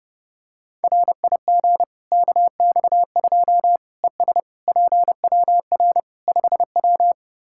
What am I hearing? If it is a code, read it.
Morse code, heard as a keyed tone, about 30 wpm, 700 Hz. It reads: RIG KX2 ES PWR 5W
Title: Practice QSO - Exchange 2 - Info - 30wpm